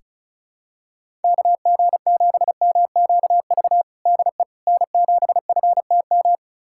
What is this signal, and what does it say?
Morse code, heard as a keyed tone, about 34 wpm, 700 Hz. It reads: KG7MQV DE N7FTM
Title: USA Callsign DE Callsign 35wpm